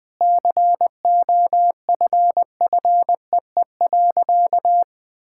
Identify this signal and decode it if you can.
Morse code, heard as a keyed tone, about 21 wpm, 700 Hz. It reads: COFFEE.
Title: Sentences with 7-Word Phrases - 20wpm